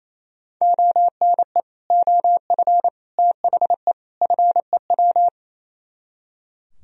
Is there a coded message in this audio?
Yes: Morse code from 0.6 to 5.3 s, heard as a keyed tone, about 38 wpm, 700 Hz. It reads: ONE OF THE FEW